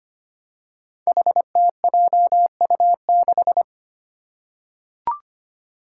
Morse code, heard as a keyed tone, about 25 wpm, 700 Hz. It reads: HTJU6
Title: Five Challenging Letter-Number Combinations 25wpm